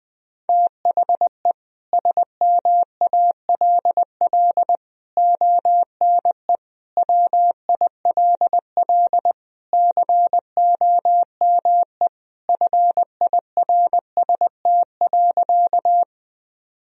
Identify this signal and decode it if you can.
Morse code, heard as a keyed tone, about 21 wpm, 700 Hz. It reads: THE SMALL ONE WILL COME FIRST.
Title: Sentences from Top 1000 Words - Review Part 1 - 20wpm